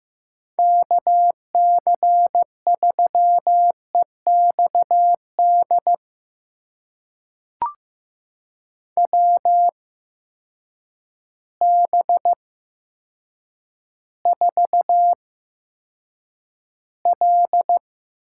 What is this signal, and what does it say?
Morse code, heard as a keyed tone, about 15 wpm, spaced as for 8 wpm, 700 Hz. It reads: KC3EXD WB4L
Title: USA Call Signs - 6 Characters - 8x Character Spacing - 15wpm